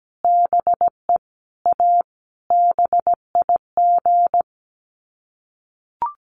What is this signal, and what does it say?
Morse code, heard as a keyed tone, about 17 wpm, 700 Hz. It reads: BE A BIG
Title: Sets of 5 Words Encore2 17wpm